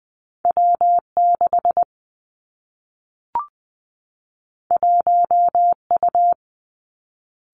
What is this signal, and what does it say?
Morse code, heard as a keyed tone, about 20 wpm, 700 Hz. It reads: W 6 1 U
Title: Two Challenging Letter-Number Combinations 20wpm